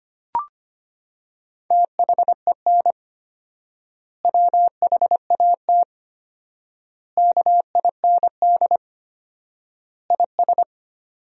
Morse code, heard as a keyed tone, about 25 wpm, 700 Hz. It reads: THEN WHAT KIND IS
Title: 4x Word-Spacing - Sentences from Top 200 Words - 25wpm